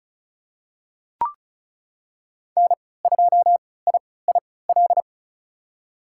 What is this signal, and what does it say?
Morse code, heard as a keyed tone, about 36 wpm, spaced as for 20 wpm, 700 Hz. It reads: N2IIL
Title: USA Call Signs - 5 Characters - 3x Character Spacing - 35wpm